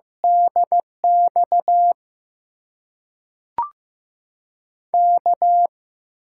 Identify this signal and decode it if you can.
Morse code, heard as a keyed tone, about 15 wpm, spaced as for 5 wpm, 700 Hz. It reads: DXK
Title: USA Call Signs - 6 Characters - 6x Character Spacing - 15wpm